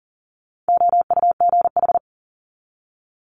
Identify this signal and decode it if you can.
Morse code, heard as a keyed tone, about 42 wpm, 700 Hz. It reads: OUGH
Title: Multi-letter Phonemes 40wpm